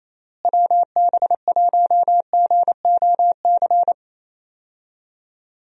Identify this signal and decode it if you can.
Morse code, heard as a keyed tone, about 27 wpm, 700 Hz. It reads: WB1GOC
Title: USA Call Signs - 6 Characters - Repeated 3x - 28wpm